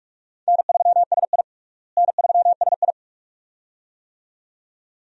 Morse code, heard as a keyed tone, about 46 wpm, 700 Hz. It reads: N3SI N3SI
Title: USA Call Signs - 4 Characters - Repeated 2x - 45wpm